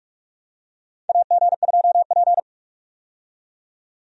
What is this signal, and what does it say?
Morse code, heard as a keyed tone, about 46 wpm, 700 Hz. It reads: AG2P
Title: USA Call Signs - 4 Characters - 45wpm